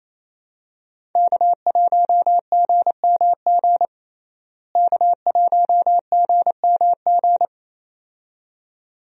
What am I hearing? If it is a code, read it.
Morse code, heard as a keyed tone, about 28 wpm, 700 Hz. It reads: K1GMG K1GMG
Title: USA Call Signs - 5 Characters - Repeated 2x - 28wpm